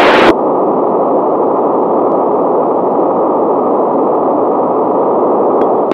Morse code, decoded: none

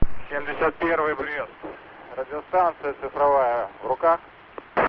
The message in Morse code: none

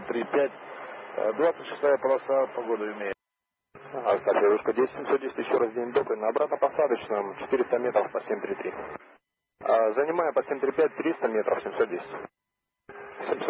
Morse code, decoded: none

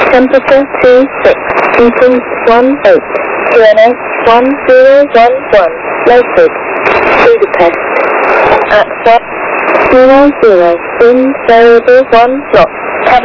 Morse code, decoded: none

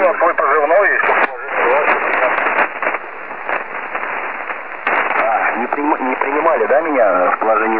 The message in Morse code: none

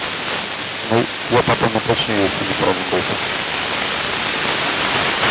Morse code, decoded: none

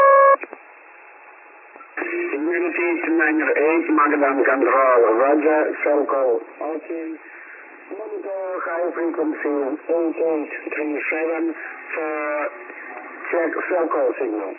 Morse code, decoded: none

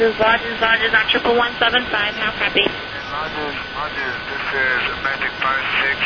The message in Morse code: none

